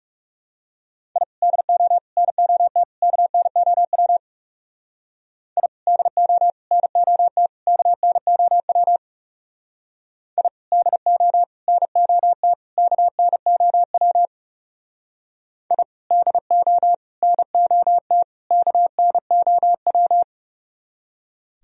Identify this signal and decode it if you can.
Morse code, heard as a keyed tone, about 39 wpm, 700 Hz. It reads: I DO NOT KNOW I DO NOT KNOW I DO NOT KNOW I DO NOT KNOW